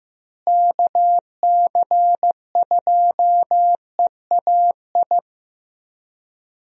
Morse code, heard as a keyed tone, about 15 wpm, 700 Hz. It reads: KC2EAI
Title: USA Call Signs - 6 Characters - Repeated 2x - 15wpm